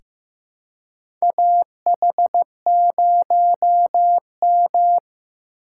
Morse code, heard as a keyed tone, about 15 wpm, 700 Hz. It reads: AH0M
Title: USA Call Signs - 4 Characters - 3x Character Spacing - 15wpm